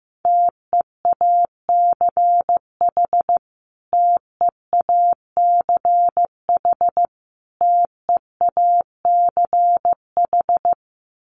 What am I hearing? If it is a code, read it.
Morse code, heard as a keyed tone, about 15 wpm, 700 Hz. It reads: TEACH TEACH TEACH
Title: Top 500 Words Repeated 15wpm